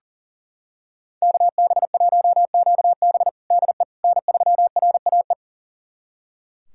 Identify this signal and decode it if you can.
Morse code, heard as a keyed tone, about 40 wpm, 700 Hz. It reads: KB1QB DE N3RAE
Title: USA Callsign DE Callsign - 5 Characters - 40wpm